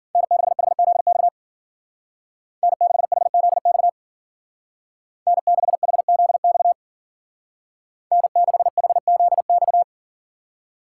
Morse code, heard as a keyed tone, about 52 wpm, 700 Hz. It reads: N6HZX N6HZX N6HZX N6HZX